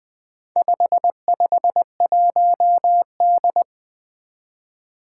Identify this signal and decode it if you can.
Morse code, heard as a keyed tone, about 20 wpm, 700 Hz. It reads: 551D
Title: Four Challenging Letter-Number Combinations 20wpm